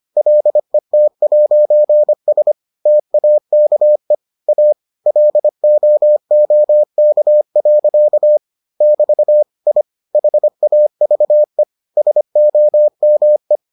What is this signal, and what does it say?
Morse code, heard as a keyed tone, about 25 wpm, 600 Hz. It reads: LET'S TAKE A LOOK. <BT> I HAVE SOME